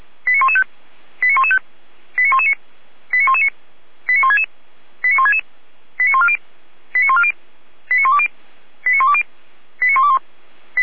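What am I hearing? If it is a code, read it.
Five-tone selective call (ZVEI), heard as quick runs of five tones, tones of about 70 ms. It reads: 89115 89115 89119 89119 89166 89166 89140 89140 89120 89120 89121
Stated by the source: ZVEI five-tone selective call 89092 received by SDR.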